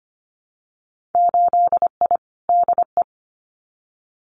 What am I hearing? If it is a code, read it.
Morse code, heard as a keyed tone, about 25 wpm, 700 Hz. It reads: 8I DE
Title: Winter Field Day Exchanges Encore 25wpm